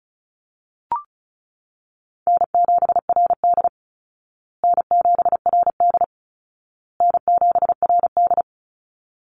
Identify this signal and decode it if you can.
Morse code, heard as a keyed tone, about 34 wpm, 700 Hz. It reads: N7RD N7RD N7RD
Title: Easy Call Signs Repeated 3x - 35wpm